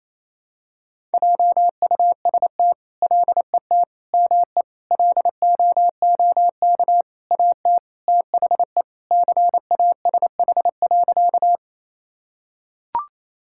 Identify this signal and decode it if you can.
Morse code, heard as a keyed tone, about 28 wpm, 700 Hz. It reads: JUST LET ME LOOK AT THE CASH.
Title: Sentences from Top 1000 Words - Review Part 5 - 28wpm